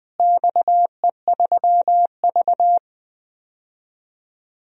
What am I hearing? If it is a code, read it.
Morse code, heard as a keyed tone, about 21 wpm, 700 Hz. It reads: XE3V
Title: DXCC-100 Call Signs - 4 Characters - Repeated 2x - 20wpm